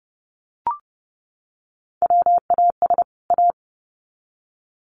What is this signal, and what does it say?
Morse code, heard as a keyed tone, about 30 wpm, 700 Hz. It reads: WAS A
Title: Sets of 2 Words 30wpm